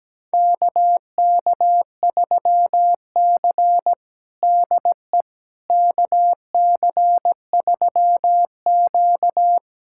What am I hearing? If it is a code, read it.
Morse code, heard as a keyed tone, about 17 wpm, 700 Hz. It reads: KK3C DE KC3Q
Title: USA Callsign DE Callsign - 4 Characters - 17wpm